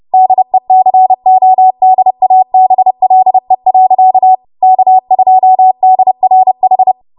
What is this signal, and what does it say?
Morse code, heard as a keyed tone, about 30 wpm, 750 Hz. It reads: DECODABLE. K2DRH
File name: Morse / 30WPM